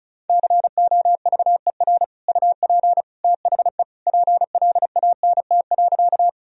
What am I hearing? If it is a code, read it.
Morse code, heard as a keyed tone, about 36 wpm, 700 Hz. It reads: COVER UP THE PLANT.
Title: Sentences from Top 1000 Words - Review Part 1 - 35wpm